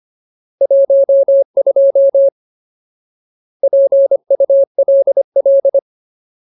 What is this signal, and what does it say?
Morse code, heard as a keyed tone, about 25 wpm, 550 Hz. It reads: 12 PULL
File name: Ultimate Word List / 25WPM Ultimate